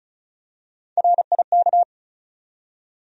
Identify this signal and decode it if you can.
Morse code, heard as a keyed tone, about 36 wpm, 700 Hz. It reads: RIK
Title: USA Call Sign Postfixes - 3 Letters 35wpm